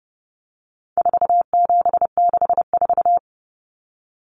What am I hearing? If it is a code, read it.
Morse code, heard as a keyed tone, about 30 wpm, 700 Hz. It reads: 4764
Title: Numbers - 4-Digit 30wpm